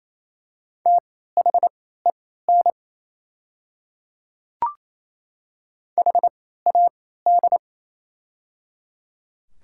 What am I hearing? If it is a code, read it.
Morse code, heard as a keyed tone, about 27 wpm, spaced as for 16 wpm, 700 Hz. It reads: THEN HAD
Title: 3x Character Spacing - Top 100 Words - 28wpm